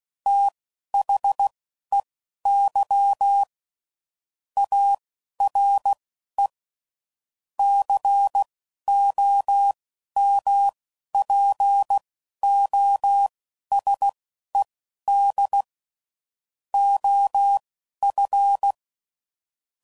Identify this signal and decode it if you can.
Morse code, heard as a keyed tone, about 16 wpm, spaced as for 11 wpm, 800 Hz. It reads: THEY ARE COMPOSED OF
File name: CW Code Groups @10WPM / Technical Subjects / SwPwrSup